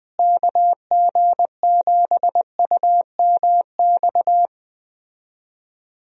Morse code, heard as a keyed tone, about 20 wpm, 700 Hz. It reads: KG7UMX
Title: USA Call Signs - 6 Characters - Repeated 3x - 20wpm